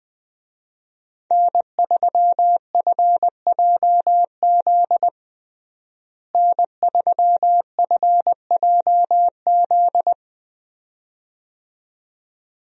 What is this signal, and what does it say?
Morse code, heard as a keyed tone, about 20 wpm, 700 Hz. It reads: N3FJZ N3FJZ
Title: USA Call Signs - 5 Characters - Repeated 2x - 20wpm